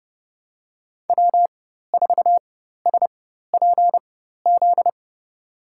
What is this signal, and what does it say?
Morse code, heard as a keyed tone, about 29 wpm, spaced as for 14 wpm, 700 Hz. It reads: W4SPZ
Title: USA Call Signs - 5 Characters - 4x Character Spacing - 30wpm